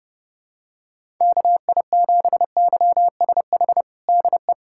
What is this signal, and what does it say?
Morse code, heard as a keyed tone, about 30 wpm, 700 Hz. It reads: KI7YSH DE
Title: USA Callsign DE Callsign 30wpm